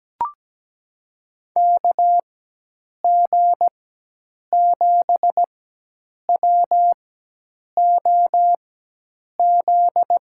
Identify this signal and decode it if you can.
Morse code, heard as a keyed tone, about 17 wpm, 700 Hz. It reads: K G 7 W O Z
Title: USA Call Signs - 6 Characters - 4x Character Spacing - 17wpm